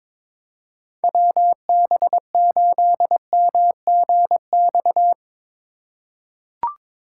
Morse code, heard as a keyed tone, about 22 wpm, 700 Hz. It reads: WB8MGX
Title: USA Call Signs - 6 Characters - 2x Character Spacing - 22wpm